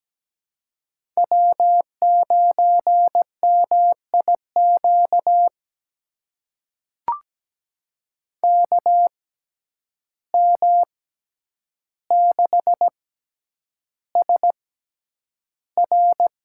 Morse code, heard as a keyed tone, about 17 wpm, spaced as for 9 wpm, 700 Hz. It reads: W9MIQ KM6SR
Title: USA Call Signs - 5 Characters - 6x Character Spacing - 17wpm